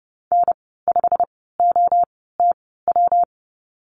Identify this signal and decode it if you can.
Morse code, heard as a keyed tone, about 30 wpm, spaced as for 17 wpm, 700 Hz. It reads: N5OTW
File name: Lesson 19g - Letter w - Call Signs 30wpm